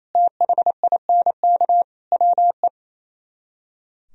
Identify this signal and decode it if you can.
Morse code, heard as a keyed tone, about 34 wpm, 700 Hz. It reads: THINK WE
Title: Speed-Racing - Sets of 4 Words 28wpm